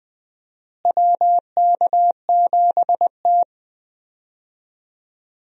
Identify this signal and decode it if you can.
Morse code, heard as a keyed tone, about 20 wpm, 700 Hz. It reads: WK7T
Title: USA Call Signs - 4 Characters - Repeated 2x - 20wpm